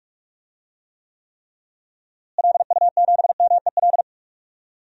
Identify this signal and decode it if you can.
Morse code, heard as a keyed tone, about 46 wpm, 700 Hz. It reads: LU7MEL
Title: DXCC-100 Call Signs - 6 Characters - Repeated 3x - 45wpm